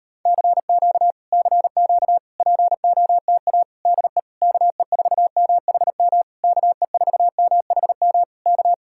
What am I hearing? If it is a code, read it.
Morse code, heard as a keyed tone, about 39 wpm, 700 Hz. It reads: CQ CQ POTA DE KE4MHM KE4MHM K